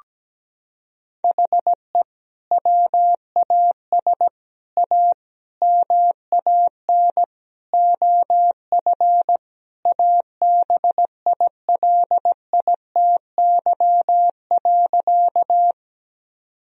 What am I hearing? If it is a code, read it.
Morse code, heard as a keyed tone, about 17 wpm, 700 Hz. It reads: HE WAS A MAN OF ABILITY.
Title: Sentences from Top 1000 Words - Review Part 3 - 17wpm